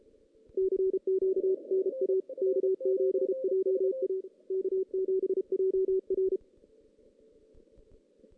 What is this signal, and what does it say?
Morse code, heard as a keyed tone, about 33 wpm, 380 Hz. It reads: CQ NA K7JA K7JR
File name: 2016 JAN CW NAQP